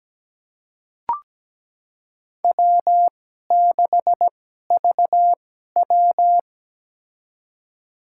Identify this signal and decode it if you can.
Morse code, heard as a keyed tone, about 17 wpm, spaced as for 12 wpm, 700 Hz. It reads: W6VW
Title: USA Call Signs - 4 Characters - 2x Character Spacing - 17wpm